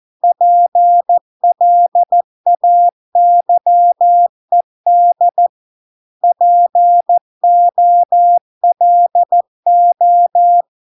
Morse code, heard as a keyed tone, about 14 wpm, 700 Hz. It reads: PLAYED POLO